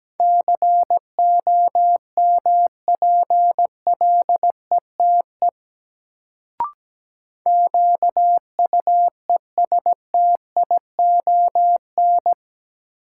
Morse code, heard as a keyed tone, about 17 wpm, 700 Hz. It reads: COMPLETE QUESTION
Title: Sentences from Top 400 Words - New Word Drill - 17wpm